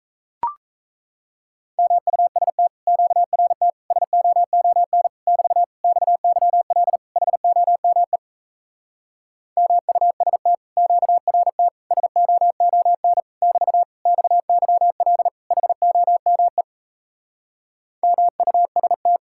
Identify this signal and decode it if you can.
Morse code, heard as a keyed tone, about 37 wpm, 700 Hz. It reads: MUST QRT SOON <BT> XYL HOME MUST QRT SOON <BT> XYL HOME MUST